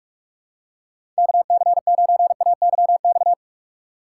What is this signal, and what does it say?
Morse code, heard as a keyed tone, about 46 wpm, 700 Hz. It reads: KC9AYX
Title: USA Call Signs - 6 Characters - 3x Character Spacing - 45wpm